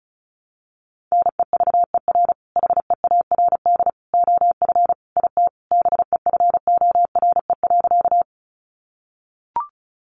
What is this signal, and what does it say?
Morse code, heard as a keyed tone, about 34 wpm, 700 Hz. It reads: NEVER HEARD OF IT BEFORE.
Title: Sentences from Top 400 Words 35wpm